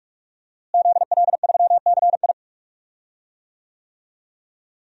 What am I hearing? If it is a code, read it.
Morse code, heard as a keyed tone, about 42 wpm, 700 Hz. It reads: ZL3CI
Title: DXCC-100 Call Signs - 5 Characters - Repeated 2x - 45wpm